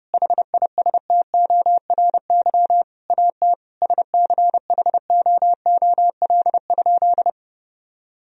Morse code, heard as a keyed tone, about 30 wpm, 700 Hz. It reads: HISTORY AT SCHOOL?
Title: Sentences from Top 1000 Words - Review Part 4 - 30wpm